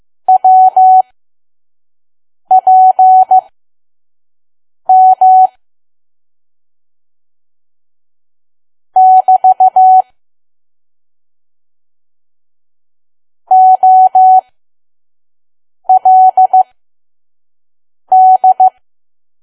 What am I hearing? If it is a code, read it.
Morse code, heard as a keyed tone, about 15 wpm, spaced as for 5 wpm, 750 Hz. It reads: WPM <BT> OLD
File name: Archive / 5 WPM files / PM